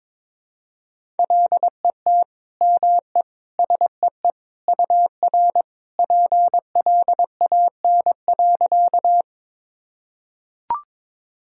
Morse code, heard as a keyed tone, about 21 wpm, 700 Hz. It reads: LET ME SEE UR PLAN.
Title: Sentences from Top 500 Words and Abbreviations 22wpm